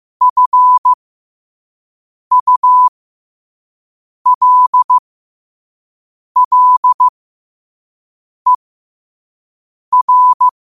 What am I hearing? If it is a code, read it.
Morse code, heard as a keyed tone, about 15 wpm, spaced as for 5 wpm, 1000 Hz. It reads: FULLER
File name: qotdfiles / 5wpm